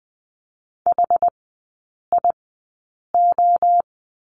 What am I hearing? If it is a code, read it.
Morse code, heard as a keyed tone, about 20 wpm, spaced as for 8 wpm, 700 Hz. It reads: HIO
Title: 060 Lesson 11e - Letter h - Three Characters 20wpm